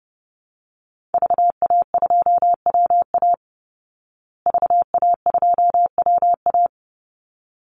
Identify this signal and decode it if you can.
Morse code, heard as a keyed tone, about 30 wpm, 700 Hz. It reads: VA2WA VA2WA
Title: Call Signs Repeated 3x - 30wpm